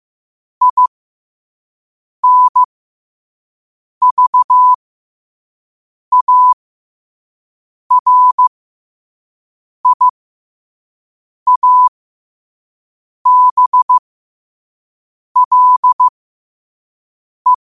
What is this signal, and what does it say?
Morse code, heard as a keyed tone, about 15 wpm, spaced as for 5 wpm, 1000 Hz. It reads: INVARIABLE
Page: New quotes every day in morse code at 5 Words per minute.